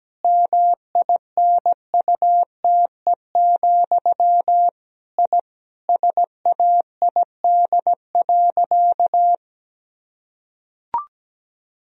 Morse code, heard as a keyed tone, about 17 wpm, 700 Hz. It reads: MINUTE, I SAID.
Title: Sentences from Top 1000 Words - Review Part 2 - 17wpm